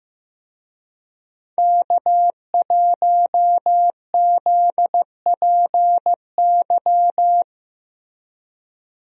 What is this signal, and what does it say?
Morse code, heard as a keyed tone, about 16 wpm, 700 Hz. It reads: K1ZPY
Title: USA Call Signs - 5 Characters - Repeated 3x - 15wpm